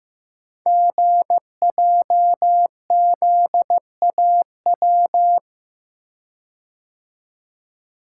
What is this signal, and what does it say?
Morse code, heard as a keyed tone, about 15 wpm, 700 Hz. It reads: GJZAW
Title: Code Groups - 5 Letters - 15wpm